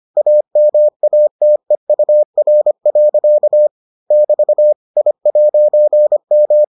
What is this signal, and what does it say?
Morse code, heard as a keyed tone, about 25 wpm, 600 Hz. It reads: AMATEUR. <BT> I'M